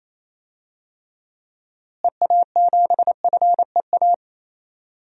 Morse code, heard as a keyed tone, about 28 wpm, 700 Hz. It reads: EA7FEA
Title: DXCC-100 Call Signs - 6 Characters - Repeated 3x - 28wpm